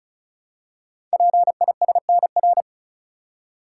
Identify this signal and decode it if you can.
Morse code, heard as a keyed tone, about 35 wpm, 690 Hz. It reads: PISNR
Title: Code Groups - 5 Letters - 35wpm